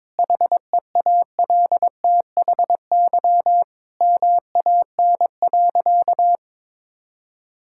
Morse code, heard as a keyed tone, about 22 wpm, 700 Hz. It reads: HEALTHY MAN.
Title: Sentences from Top 1000 Words - Review Part 5 - 22wpm